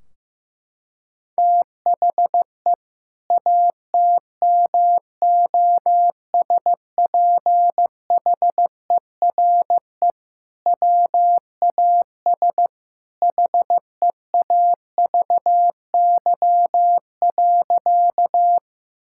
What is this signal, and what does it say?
Morse code, heard as a keyed tone, about 16 wpm, 700 Hz. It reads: THE ATMOSPHERE WAS HEAVY.
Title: Sentences from Top 1600 Words 15wpm